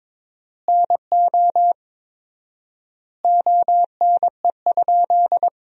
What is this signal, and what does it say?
Morse code, heard as a keyed tone, about 21 wpm, 700 Hz. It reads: NO ONE?
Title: 4x Word-Spacing - Sentences from Top 300 Words - 22wpm